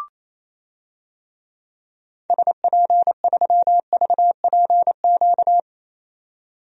Morse code, heard as a keyed tone, about 28 wpm, 700 Hz. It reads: SP3VPQ
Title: DXCC-100 Call Signs - 6 Characters - 28wpm